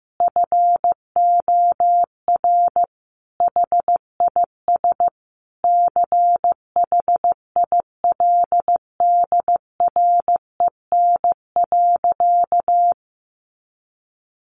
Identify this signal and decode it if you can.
Morse code, heard as a keyed tone, about 15 wpm, 700 Hz. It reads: FOR HIS CHILDREN.
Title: Sentences from Top 1100 Words 15wpm